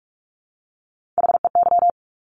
Morse code, heard as a keyed tone, about 46 wpm, 700 Hz. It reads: HEY
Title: Top Three Letter Combinations in Words 45wpm